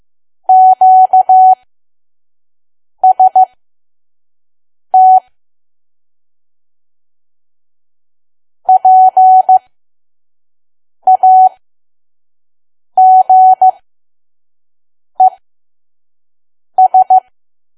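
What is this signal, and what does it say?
Morse code, heard as a keyed tone, about 15 wpm, spaced as for 5 wpm, 750 Hz. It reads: QST PAGES
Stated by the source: Morse code practice at 5 WPM from KQ2P.COM.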